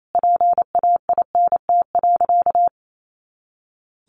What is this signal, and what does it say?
Morse code, heard as a keyed tone, about 28 wpm, 700 Hz. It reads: PAINT.
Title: A Sound Of Thunder 28wpm